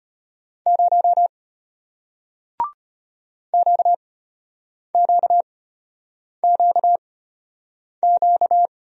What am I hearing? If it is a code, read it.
Morse code, heard as a keyed tone, about 33 wpm, spaced as for 8 wpm, 700 Hz. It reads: 0 QQQQ